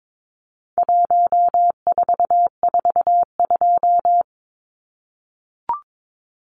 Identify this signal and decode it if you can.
Morse code, heard as a keyed tone, about 22 wpm, 700 Hz. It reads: 1442
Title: Numbers - 4-Digit 22wpm